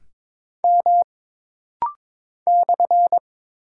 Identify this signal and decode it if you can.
Morse code, heard as a keyed tone, about 22 wpm, 700 Hz. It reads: M /
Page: This rapid-fire timing is geared to push your ICR (Instant Character Recognition) skills.